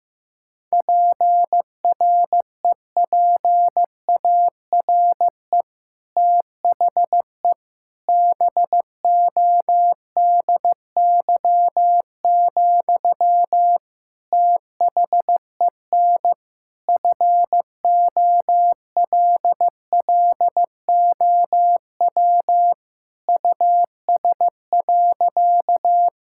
Morse code, heard as a keyed tone, about 15 wpm, 700 Hz. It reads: PREPARE THE BODY, THEN FOLLOW US.
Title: Sentences from Top 1000 Words - Review Part 6 - 15wpm